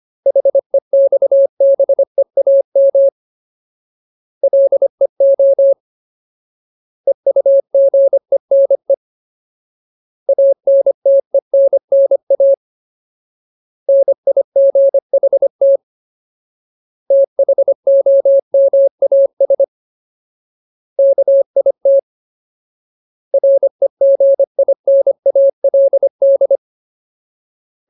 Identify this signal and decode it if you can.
Morse code, heard as a keyed tone, about 25 wpm, 550 Hz. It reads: HEXBEAM LEO EUGENE ANTENNA NIGHT THOMAS KIT REGINALD
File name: Ultimate Word List / 25WPM Ultimate